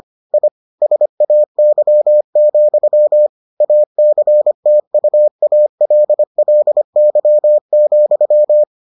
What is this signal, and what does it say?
Morse code, heard as a keyed tone, about 25 wpm, 600 Hz. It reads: I SAY, ACTUALLY,